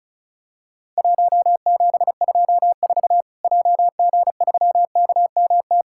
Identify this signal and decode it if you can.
Morse code, heard as a keyed tone, about 34 wpm, 700 Hz. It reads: 1724 JG3KMT
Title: Worked All Europe DX Contest - QTC Exchange - 35wpm